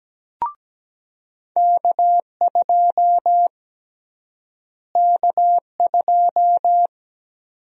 Morse code, heard as a keyed tone, about 17 wpm, 700 Hz. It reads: K2 K2
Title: USA Call Sign Prefixes - Repeated 2x - 17wpm